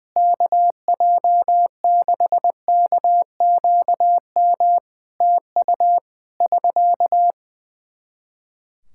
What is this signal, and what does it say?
Morse code, heard as a keyed tone, about 20 wpm, 700 Hz. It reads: KJ6KQM TU <SK>
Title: Practice QSO - Exchange 4 - Ending - 20wpm